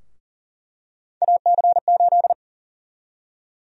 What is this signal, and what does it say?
Morse code, heard as a keyed tone, about 37 wpm, 700 Hz. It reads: AC8
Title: USA Call Sign Prefixes - 40wpm